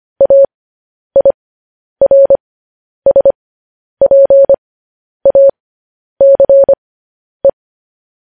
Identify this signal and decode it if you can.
Morse code, heard as a keyed tone, about 25 wpm, spaced as for 10 wpm, 550 Hz. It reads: AIRSPACE
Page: News Headlines in Morse Code at 10 WPM